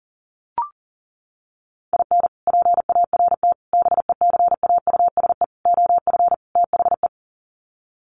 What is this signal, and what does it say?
Morse code, heard as a keyed tone, about 40 wpm, 700 Hz. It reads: IN PART BECAUSE OF THE
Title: Sets of 5 Words Encore2 40wpm